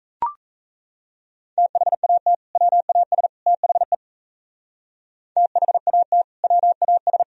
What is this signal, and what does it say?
Morse code, heard as a keyed tone, about 37 wpm, 700 Hz. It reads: THAT WAS THE THAT WAS